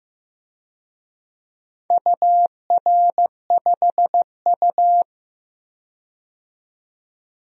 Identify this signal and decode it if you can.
Morse code, heard as a keyed tone, about 15 wpm, 700 Hz. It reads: UR5U
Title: DXCC-100 Call Signs - 4 Characters - 15wpm